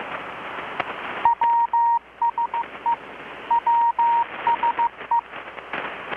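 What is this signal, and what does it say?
Morse code, heard as a keyed tone, about 15 wpm, 940 Hz. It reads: WSE WSE
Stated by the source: Morse from KPH's Night of Nights XX maritime radiotelegraphy event, received in Chiapas, Mexico about 3,800 km away on a Kenwood R-600 with a grounded 10 m long-wire antenna.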